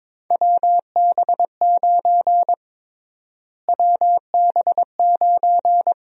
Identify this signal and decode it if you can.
Morse code, heard as a keyed tone, about 22 wpm, 700 Hz. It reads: WB9 WB9
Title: USA Call Sign Prefixes - Repeated 2x - 22wpm